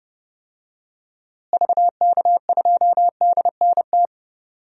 Morse code, heard as a keyed tone, about 32 wpm, 700 Hz. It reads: VK2DNT
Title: DXCC-100 Call Signs - 6 Characters - Repeated 3x - 30wpm